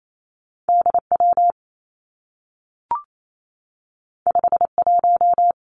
Morse code, heard as a keyed tone, about 28 wpm, 700 Hz. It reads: D W 5 1
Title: Two Challenging Letter-Number Combinations 28wpm